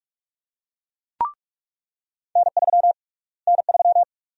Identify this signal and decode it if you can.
Morse code, heard as a keyed tone, about 42 wpm, 700 Hz. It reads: N3 N3
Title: USA Call Sign Prefixes - Repeated 2x - 45wpm